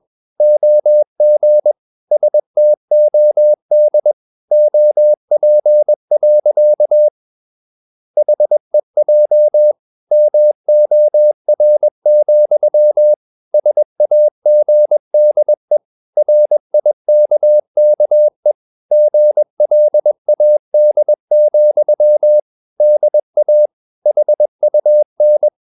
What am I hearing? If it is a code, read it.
Morse code, heard as a keyed tone, about 21 wpm, 600 Hz. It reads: OG STOD OP. HEJ MOR, SAGDE RIKKE GLAD, DA HUN
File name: Ferie QRS 21wpm